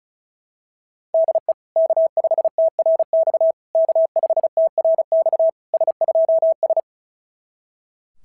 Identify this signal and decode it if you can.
Morse code, heard as a keyed tone, about 36 wpm, 660 Hz. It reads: DE K5TRX K5TRX S2S
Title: SOTA QSO Practice 35wpm